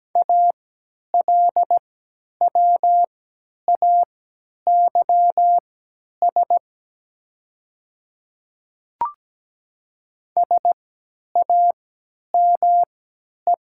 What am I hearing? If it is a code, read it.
Morse code, heard as a keyed tone, about 17 wpm, spaced as for 10 wpm, 700 Hz. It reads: ALWAYS SAME
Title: 3x Character Spacing - Top 500 Words - 17wpm